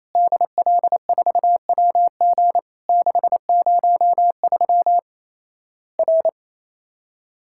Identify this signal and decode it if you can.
Morse code, heard as a keyed tone, about 28 wpm, 700 Hz. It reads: DL4WG 603 R
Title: Worked All Europe DX Contest - QTC Exchange - 28wpm